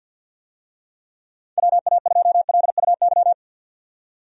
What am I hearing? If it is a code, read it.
Morse code, heard as a keyed tone, about 53 wpm, 700 Hz. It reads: WA2LUY
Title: USA Call Signs - 6 Characters - 50wpm